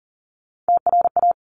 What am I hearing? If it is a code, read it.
Morse code, heard as a keyed tone, about 39 wpm, 700 Hz. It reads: TRA
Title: Top Three Letter Combinations in Words 40wpm